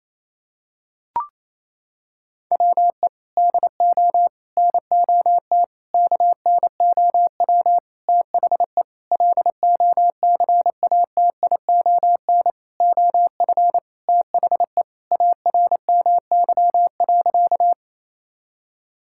Morse code, heard as a keyed tone, about 28 wpm, 700 Hz. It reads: WE DO NOT KNOW THE LOCATION OF THE ARMY.
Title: Sentences from Top 1000 Words - Review Part 6 - 28wpm